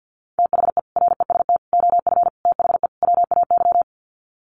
Morse code, heard as a keyed tone, about 50 wpm, 700 Hz. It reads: THE REST OF THE WAY
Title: Sets of 5 Words Encore 50wpm